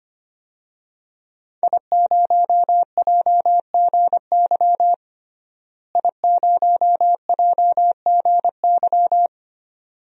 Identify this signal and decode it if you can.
Morse code, heard as a keyed tone, about 25 wpm, 700 Hz. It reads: I0JGY I0JGY
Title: DXCC-100 Call Signs - 5 Characters - Repeated 2x - 25wpm